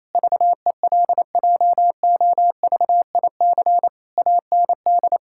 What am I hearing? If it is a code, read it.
Morse code, heard as a keyed tone, about 28 wpm, 700 Hz. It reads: VELJOVIC AND
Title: Pangram Challenge v2 - 28wpm